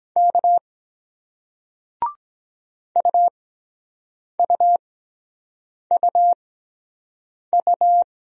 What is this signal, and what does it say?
Morse code, heard as a keyed tone, about 23 wpm, spaced as for 7 wpm, 700 Hz. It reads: K UUUU